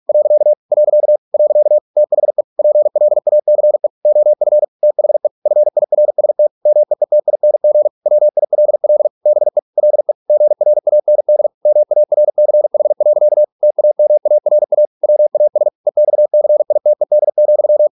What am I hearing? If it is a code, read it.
Morse code, heard as a keyed tone, about 46 wpm, 600 Hz. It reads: . . THE PLACE OF THE FIRST MEETING WILL BE LE GRAND MARCH. TAMARA WAS EXCITED,